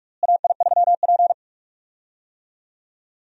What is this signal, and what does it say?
Morse code, heard as a keyed tone, about 48 wpm, 700 Hz. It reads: AI3P
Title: USA Call Signs - 4 Characters - Repeated 3x - 45wpm